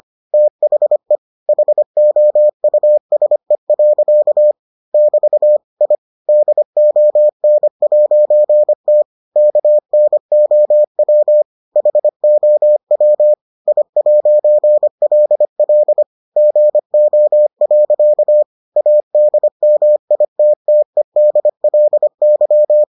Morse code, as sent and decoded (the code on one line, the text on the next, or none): - .... / .... --- ..- ... .-.-.- / -...- / .. / -.. --- -. .----. - / -.- -. --- .-- / .... --- .-- / .. .----. .-.. .-.. / --. --- .-.-.- / .- -.. -- .. - - . -.. .-.. -.--
THE HOUSE. <BT> I DON'T KNOW HOW I'LL GO. ADMITTEDLY